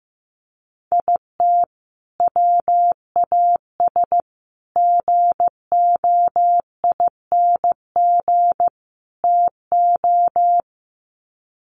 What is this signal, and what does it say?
Morse code, heard as a keyed tone, about 15 wpm, 700 Hz. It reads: IT WAS GOING TO
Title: Sets of 4 Words 15wpm